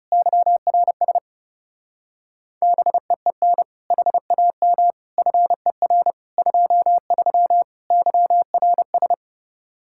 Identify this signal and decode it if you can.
Morse code, heard as a keyed tone, about 36 wpm, 700 Hz. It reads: YRS BEEN HAM FER 23 YRS
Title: Speed-Racing Common QSO Phrases - 30wpm